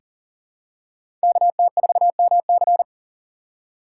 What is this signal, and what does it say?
Morse code, heard as a keyed tone, about 39 wpm, 700 Hz. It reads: KT4MC
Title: USA Call Signs - 8x Character Spacing - 40wpm